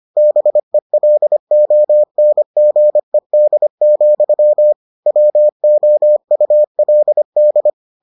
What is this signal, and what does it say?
Morse code, heard as a keyed tone, about 25 wpm, 600 Hz. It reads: BELONGED, WOULD